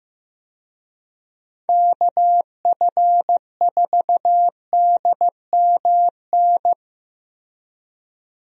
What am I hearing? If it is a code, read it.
Morse code, heard as a keyed tone, about 15 wpm, 700 Hz. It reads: KF4DMN
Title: USA Call Signs - 6 Characters - Repeated 3x - 15wpm